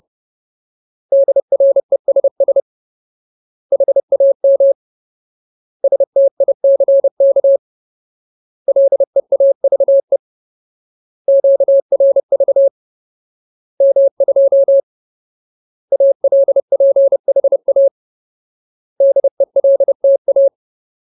Morse code, heard as a keyed tone, about 30 wpm, 550 Hz. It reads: DRESS HAM STICK LEAVE QRV M2 ALPHA DELTA
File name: Ultimate Word List / 30WPM Ultimate